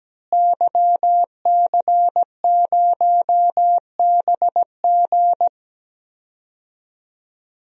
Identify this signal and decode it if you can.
Morse code, heard as a keyed tone, about 17 wpm, 700 Hz. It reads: YC0BG
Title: DXCC-100 Call Signs - 5 Characters - Repeated 2x - 17wpm